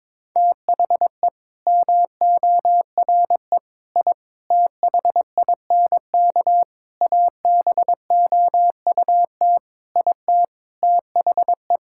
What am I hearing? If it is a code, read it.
Morse code, heard as a keyed tone, about 22 wpm, 700 Hz. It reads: THE MORE I THINK ABOUT IT THE
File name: practice / 22wpm / Sets of 8 Words - 22wpm